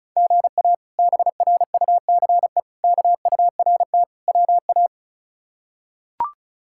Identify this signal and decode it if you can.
Morse code, heard as a keyed tone, about 36 wpm, 700 Hz. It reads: GA BRUCE KURT WA
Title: K1USN SST QSO Practice - 35wpm